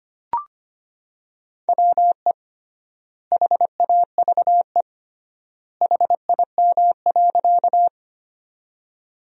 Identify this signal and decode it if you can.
Morse code, heard as a keyed tone, about 25 wpm, 700 Hz. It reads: WE HAVE HIM.
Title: 3x Word-Spacing - Sentences from Top 100 Words - 25wpm